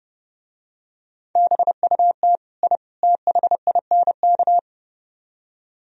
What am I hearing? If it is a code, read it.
Morse code, heard as a keyed tone, about 28 wpm, 700 Hz. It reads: BUT I THINK